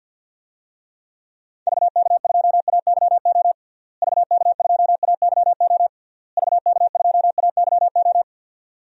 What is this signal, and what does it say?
Morse code, heard as a keyed tone, about 52 wpm, 700 Hz. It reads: VK2AYO VK2AYO VK2AYO
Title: DXCC-100 Call Signs - 6 Characters - Repeated 3x - 50wpm